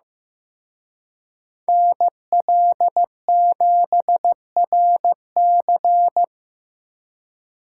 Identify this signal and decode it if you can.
Morse code, heard as a keyed tone, about 18 wpm, 700 Hz. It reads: NL7RC
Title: Speed Racing - USA 2x2 Call Signs 15wpm